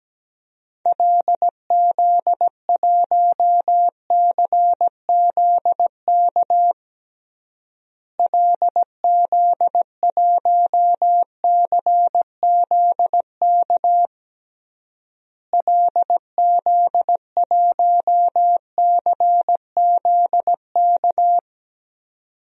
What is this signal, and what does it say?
Morse code, heard as a keyed tone, about 17 wpm, 700 Hz. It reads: LZ1CZK LZ1CZK LZ1CZK
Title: DXCC-100 Call Signs - 6 Characters - Repeated 3x - 17wpm